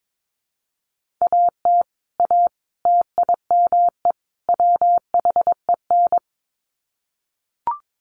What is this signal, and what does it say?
Morse code, heard as a keyed tone, about 22 wpm, 700 Hz. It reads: AT A TIME WHEN
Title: Sets of 4 Words 22wpm